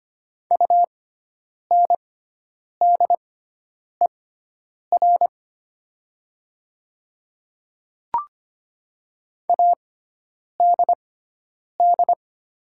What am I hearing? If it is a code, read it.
Morse code, heard as a keyed tone, about 25 wpm, spaced as for 9 wpm, 690 Hz. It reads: UNDER ADD
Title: 6x Character Spacing - Top 200 Words - 25wpm